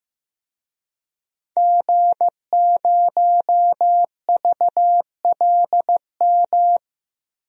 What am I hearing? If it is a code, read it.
Morse code, heard as a keyed tone, about 15 wpm, 700 Hz. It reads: G0VLM
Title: DXCC-100 Call Signs - 5 Characters - Repeated 2x - 15wpm